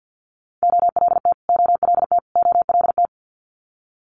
Morse code, heard as a keyed tone, about 50 wpm, 690 Hz. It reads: OLT OLT OLT
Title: Multi-letter Phonemes Repeated 3x 50wpm